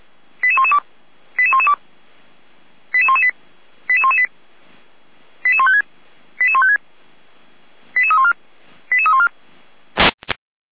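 Five-tone selective call (ZVEI), heard as quick runs of five tones, tones of about 80 ms. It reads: 88112 88112 88118 88118 88156 88156 88314 88314
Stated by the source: ZVEI five-tone call 88517 received by SDR, from audio.f85360.feuerwehr-orlamuende.org fire brigade (Feuerwehr Orlamünde) archive.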